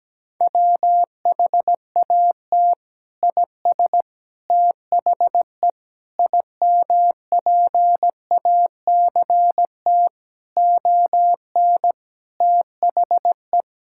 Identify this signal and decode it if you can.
Morse code, heard as a keyed tone, about 17 wpm, 700 Hz. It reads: WHAT IS THE IMPACT ON THE
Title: Sentences from Top 1000 Words - Review Part 6 - 17wpm